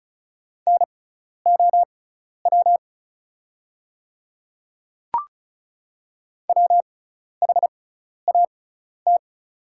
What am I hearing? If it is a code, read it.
Morse code, heard as a keyed tone, about 36 wpm, spaced as for 12 wpm, 700 Hz. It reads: NOW WHAT
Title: 6x Character Spacing - Top 300 Words - 35wpm